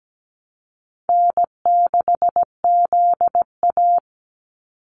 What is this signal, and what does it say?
Morse code, heard as a keyed tone, about 17 wpm, 700 Hz. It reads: N6ZA
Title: NA SOTA Activator Callsigns 17wpm